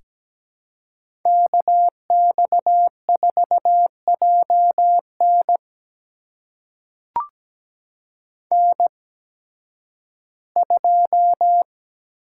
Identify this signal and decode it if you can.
Morse code, heard as a keyed tone, about 17 wpm, spaced as for 11 wpm, 700 Hz. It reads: KX4JN N2
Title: USA Call Signs - 5 Characters - 8x Character Spacing - 17wpm